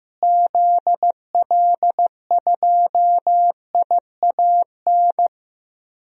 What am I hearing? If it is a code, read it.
Morse code, heard as a keyed tone, about 15 wpm, 700 Hz. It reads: ZL2IAN
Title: DXCC-100 Call Signs - 6 Characters - Repeated 3x - 15wpm